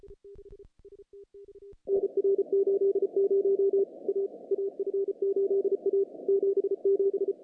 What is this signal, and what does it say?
Morse code, heard as a keyed tone, about 34 wpm, 410 Hz. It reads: NR 80 A AF8A 77
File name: contests / CW-SS-k5tr- -0000